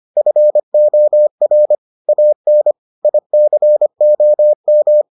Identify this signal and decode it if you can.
Morse code, heard as a keyed tone, about 25 wpm, 600 Hz. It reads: FOR AN ICOM